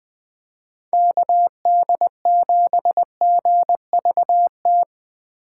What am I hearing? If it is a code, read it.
Morse code, heard as a keyed tone, about 20 wpm, 700 Hz. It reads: KD7GVT
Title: USA Call Signs - 6 Characters - 6x Character Spacing - 20wpm